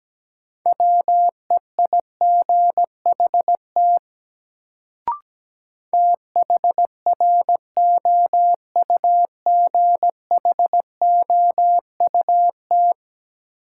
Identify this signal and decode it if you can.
Morse code, heard as a keyed tone, about 17 wpm, 700 Hz. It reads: WEIGHT THROUGHOUT
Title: Sentences from Top 800 Words - New Word Drill - 17wpm